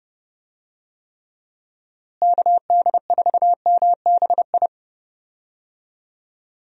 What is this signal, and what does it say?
Morse code, heard as a keyed tone, about 32 wpm, 700 Hz. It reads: KD4MBI
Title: USA Call Signs - 6 Characters - Repeated 2x - 30wpm